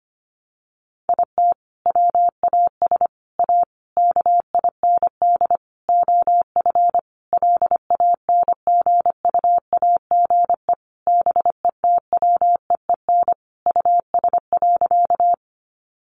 Morse code, heard as a keyed tone, about 25 wpm, 700 Hz. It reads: IT WAS A KIND OF LANGUAGE BETWEEN US.
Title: Sentences from Top 500 Words 25wpm